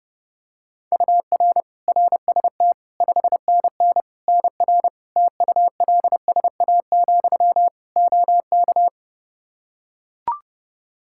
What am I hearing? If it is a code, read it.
Morse code, heard as a keyed tone, about 32 wpm, 700 Hz. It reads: UR RST 5NN NR TULSA, OK
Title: Practice QSO - Exchange 1 - Beginning - 30wpm